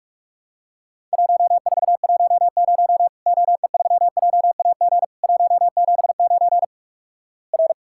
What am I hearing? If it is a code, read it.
Morse code, heard as a keyed tone, about 48 wpm, 700 Hz. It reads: 1410 OE3JAG 179 R
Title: Worked All Europe DX Contest - QTC Exchange - 45wpm